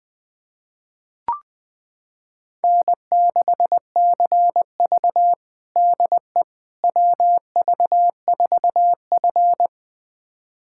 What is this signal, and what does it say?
Morse code, heard as a keyed tone, about 20 wpm, 700 Hz. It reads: N6CV DE WV4F
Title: USA Callsign DE Callsign - 4 Characters - 20wpm